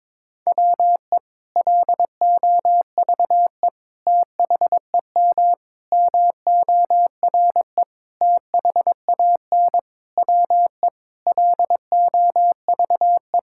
Morse code, heard as a keyed tone, about 22 wpm, 700 Hz. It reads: WE LOVE THEM MORE THAN WE LOVE
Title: Sentences from Top 1600 Words 22wpm